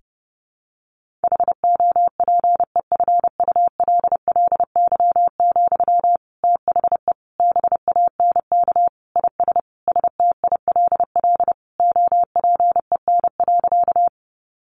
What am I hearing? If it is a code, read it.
Morse code, heard as a keyed tone, about 28 wpm, 700 Hz. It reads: HOPEFULLY, THE BANK IS STILL OPEN.
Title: Sentences from Top 1500 Words 30wpm